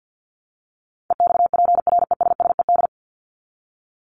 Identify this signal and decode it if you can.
Morse code, heard as a keyed tone, about 50 wpm, 700 Hz. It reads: EXPRESSED
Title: ED Suffix - Top 100 Words with Base Words 50wpm